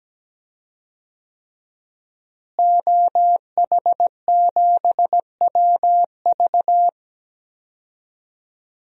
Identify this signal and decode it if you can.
Morse code, heard as a keyed tone, about 17 wpm, 700 Hz. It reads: OH7WV
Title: DXCC-100 Call Signs - 5 Characters - Repeated 3x - 17wpm